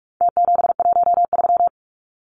Morse code, heard as a keyed tone, about 44 wpm, 700 Hz. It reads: T713
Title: Numbers - Contest Style - 4-Digit 45wpm